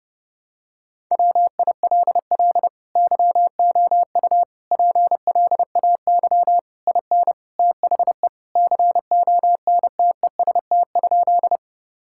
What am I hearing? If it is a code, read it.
Morse code, heard as a keyed tone, about 30 wpm, 700 Hz. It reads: WILL YOU PLAY IN THE CONTEST?
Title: Sentences from Top 1600 Words 30wpm